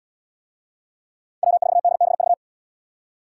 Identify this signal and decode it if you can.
Morse code, heard as a keyed tone, about 77 wpm, 700 Hz. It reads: W4DBV